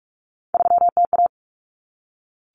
Morse code, heard as a keyed tone, about 44 wpm, 700 Hz. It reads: 3TA
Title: Common Cut Numbers - 3-Digit 45wpm